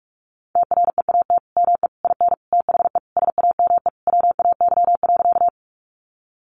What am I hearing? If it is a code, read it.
Morse code, heard as a keyed tone, about 44 wpm, 700 Hz. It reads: TREAT ME IN THE SAME WAY.
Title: Sentences from Top 1300 Words 45wpm